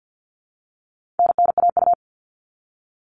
Morse code, heard as a keyed tone, about 50 wpm, 700 Hz. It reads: NNAU
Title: All Cut Numbers - 4-Digit 50wpm